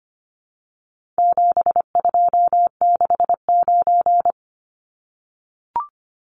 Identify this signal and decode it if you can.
Morse code, heard as a keyed tone, about 25 wpm, 700 Hz. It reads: 7269
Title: Numbers - 4-Digit 25wpm